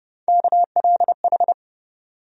Morse code, heard as a keyed tone, about 32 wpm, 700 Hz. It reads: KLH
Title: USA Call Sign Postfixes - 3 Letters 30wpm